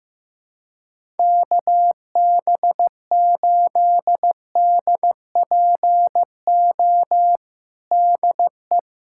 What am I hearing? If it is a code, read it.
Morse code, heard as a keyed tone, about 15 wpm, 700 Hz. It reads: KB8DPO DE
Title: USA Callsign DE Callsign - 6 Characters - 15wpm